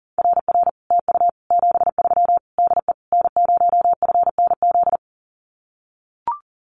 Morse code, heard as a keyed tone, about 40 wpm, 700 Hz. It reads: RR TU 73 DE N0FNZ
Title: Common QSO Phrases Encore 40wpm